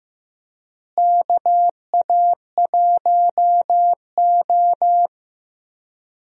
Morse code, heard as a keyed tone, about 15 wpm, 700 Hz. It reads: KA1O
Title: USA Call Signs - 4 Characters - Repeated 3x - 15wpm